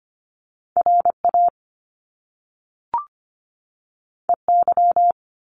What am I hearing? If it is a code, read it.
Morse code, heard as a keyed tone, about 25 wpm, 700 Hz. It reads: RA EY
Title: Top Two Letter Combinations in Words 25wpm